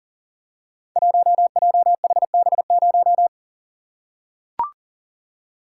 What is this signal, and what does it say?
Morse code, heard as a keyed tone, about 40 wpm, 700 Hz. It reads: 1JHB0
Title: Code Groups - 5 Letter:Numbers - 40wpm